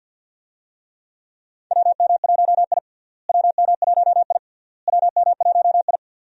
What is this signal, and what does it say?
Morse code, heard as a keyed tone, about 48 wpm, 700 Hz. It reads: WM1I WM1I WM1I
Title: USA Call Signs - 4 Characters - Repeated 3x - 50wpm